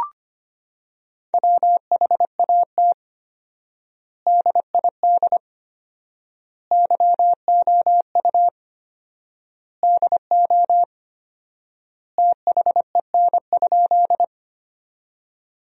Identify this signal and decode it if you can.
Morse code, heard as a keyed tone, about 25 wpm, 700 Hz. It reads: WHAT DID YOU DO THEN?
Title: 4x Word-Spacing - Sentences from Top 100 Words - 25wpm